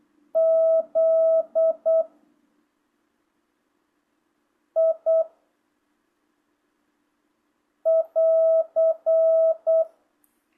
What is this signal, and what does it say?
Morse code, heard as a keyed tone, about 8 wpm, 660 Hz. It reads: Z I <AR>
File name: jh8cbh / 2025cw